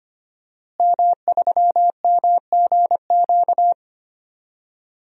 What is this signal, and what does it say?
Morse code, heard as a keyed tone, about 25 wpm, 700 Hz. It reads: M3MGQ
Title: DXCC-100 Call Signs - 5 Characters - Repeated 3x - 25wpm